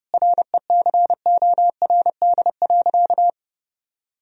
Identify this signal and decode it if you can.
Morse code, heard as a keyed tone, about 32 wpm, 700 Hz. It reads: RECORD.
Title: Sentences from Top 1000 Words - Review Part 3 - 30wpm